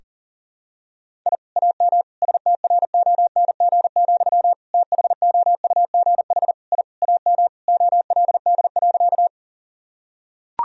Morse code, heard as a keyed tone, about 37 wpm, 700 Hz. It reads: I AM STRONG, THOUGH I AM OLD.
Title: Sentences from Top 1000 Words - Review Part 3 - 40wpm